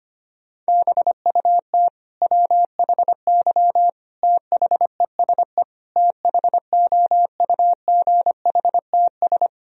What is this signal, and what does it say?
Morse code, heard as a keyed tone, about 25 wpm, 700 Hz. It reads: BUT WHY THESE THOUGHTS
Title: Sentences from Top 1000 Words - Review Part 1 - 25wpm